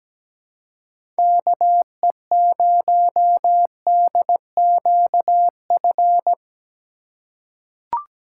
Morse code, heard as a keyed tone, about 17 wpm, 700 Hz. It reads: KE0DQF
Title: USA Call Signs - 6 Characters - 3x Character Spacing - 17wpm